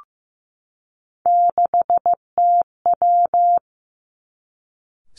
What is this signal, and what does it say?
Morse code, heard as a keyed tone, about 15 wpm, 700 Hz. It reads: 6TW
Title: Three Challenging Letter-Number Combinations 15wpm